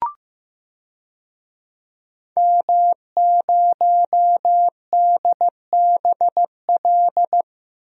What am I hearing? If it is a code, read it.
Morse code, heard as a keyed tone, about 15 wpm, 700 Hz. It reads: M0DBL
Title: DXCC-100 Call Signs - 5 Characters - 15wpm